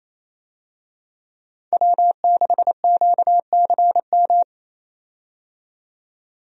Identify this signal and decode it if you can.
Morse code, heard as a keyed tone, about 28 wpm, 700 Hz. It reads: W6QCM
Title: USA Call Signs - 5 Characters - 28wpm